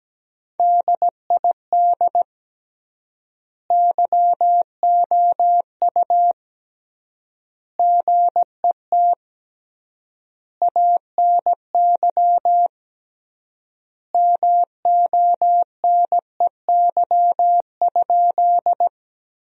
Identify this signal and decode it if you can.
Morse code, heard as a keyed tone, about 17 wpm, 700 Hz. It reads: DID YOU GET ANY MONEY?
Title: 3x Word-Spacing - Sentences from Top 400 Words - 17wpm